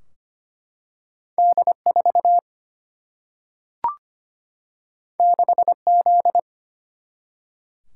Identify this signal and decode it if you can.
Morse code, heard as a keyed tone, about 25 wpm, 700 Hz. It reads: D4 6Z
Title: Single-Letter Single-Number 25wpm